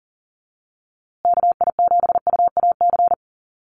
Morse code, heard as a keyed tone, about 40 wpm, 700 Hz. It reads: KI7UAC
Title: USA Call Signs Repeated 3x 40wpm